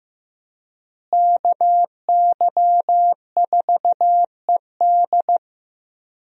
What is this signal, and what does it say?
Morse code, heard as a keyed tone, about 15 wpm, 700 Hz. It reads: KY4ED
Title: USA Call Signs - 5 Characters - Repeated 2x - 15wpm